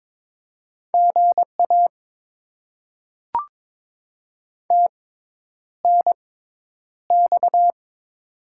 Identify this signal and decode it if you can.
Morse code, heard as a keyed tone, about 22 wpm, spaced as for 10 wpm, 700 Hz. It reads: GA TNX
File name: Top 20 QSO Elements - 6x Character Spacing - 22wpm